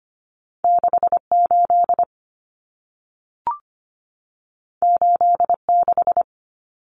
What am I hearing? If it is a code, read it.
Morse code, heard as a keyed tone, about 25 wpm, 700 Hz. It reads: 68 86
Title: Common Cut Numbers - 2-Digit 25wpm